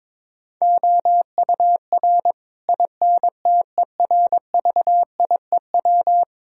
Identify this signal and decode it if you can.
Morse code, heard as a keyed tone, about 23 wpm, 700 Hz. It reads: OUR INTERVIEW
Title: Sentences from Top 1600 Words 22wpm